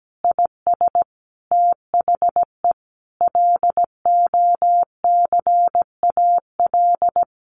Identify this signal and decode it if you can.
Morse code, heard as a keyed tone, about 17 wpm, 700 Hz. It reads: IS THE LOCAL
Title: Sentences from Top 1500 Words 17wpm